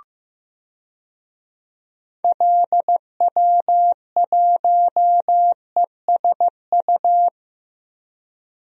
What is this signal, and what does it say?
Morse code, heard as a keyed tone, about 16 wpm, 700 Hz. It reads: LW1ESU
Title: DXCC-100 Call Signs - 6 Characters - Repeated 3x - 15wpm